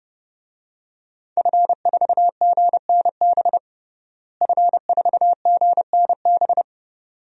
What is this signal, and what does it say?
Morse code, heard as a keyed tone, about 32 wpm, 700 Hz. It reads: F4GNB F4GNB
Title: DXCC-100 Call Signs - 5 Characters - Repeated 2x - 30wpm